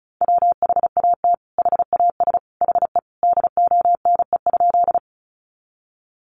Morse code, heard as a keyed tone, about 34 wpm, 700 Hz. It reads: WHAT HAS HE DONE?
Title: Sentences from Top 500 Words 35wpm